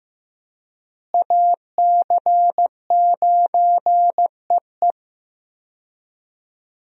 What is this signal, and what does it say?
Morse code, heard as a keyed tone, about 15 wpm, 700 Hz. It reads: AC9EE
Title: USA Call Signs - 5 Characters - Repeated 3x - 15wpm